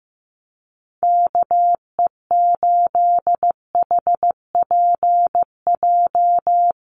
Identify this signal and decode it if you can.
Morse code, heard as a keyed tone, about 15 wpm, 700 Hz. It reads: KE8HPJ
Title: USA Call Signs Repeated 3x 15wpm